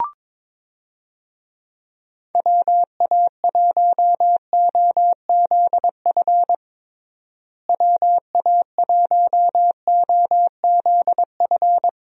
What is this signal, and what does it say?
Morse code, heard as a keyed tone, about 21 wpm, 700 Hz. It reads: WA1OZF WA1OZF
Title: USA Call Signs - 6 Characters - Repeated 3x - 22wpm